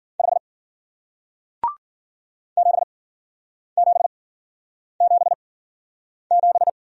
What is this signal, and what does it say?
Morse code, heard as a keyed tone, about 49 wpm, spaced as for 9 wpm, 700 Hz. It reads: 5 7777
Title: Speed-Racing Single Letter-Number 40wpm